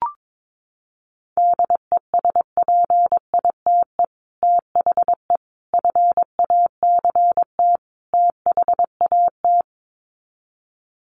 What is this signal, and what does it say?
Morse code, heard as a keyed tone, about 22 wpm, 700 Hz. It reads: DESPITE THE FACT THAT
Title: Sets of 4 Words 22wpm